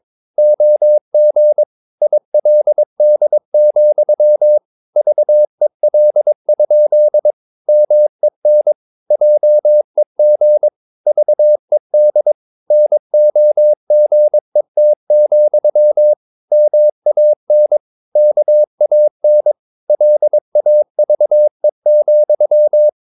Morse code, heard as a keyed tone, about 22 wpm, 600 Hz. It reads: OG ILD, VEL? MEN JEG VED NOGET, MAN KAN LAVE,
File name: Ferie QRS 21wpm